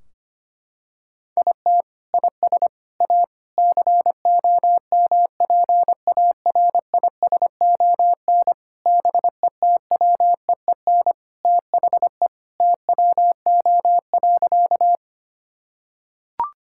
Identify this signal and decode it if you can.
Morse code, heard as a keyed tone, about 25 wpm, 700 Hz. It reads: IT IS A COMPARISON BETWEEN THE TWO.
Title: Sentences from Top 1600 Words 25wpm